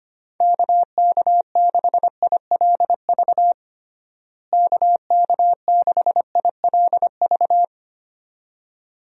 Morse code, heard as a keyed tone, about 25 wpm, 700 Hz. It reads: KK6ILV KK6ILV
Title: USA Call Signs - 6 Characters - Repeated 2x - 25wpm